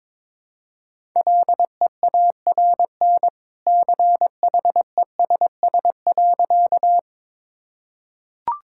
Morse code, heard as a keyed tone, about 22 wpm, 700 Hz. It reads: LEARN CHESS.
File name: practice / 22wpm / Easy Sentences with Hints - 22wpm